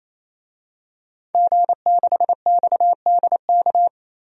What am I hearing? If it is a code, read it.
Morse code, heard as a keyed tone, about 27 wpm, 700 Hz. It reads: G6XDK
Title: DXCC-100 Call Signs - 5 Characters - 28wpm